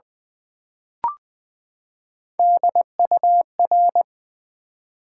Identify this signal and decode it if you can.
Morse code, heard as a keyed tone, about 20 wpm, 700 Hz. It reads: DUR
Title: Build A Word - 3 Letters At A Time - 6-Letter Words - 20wpm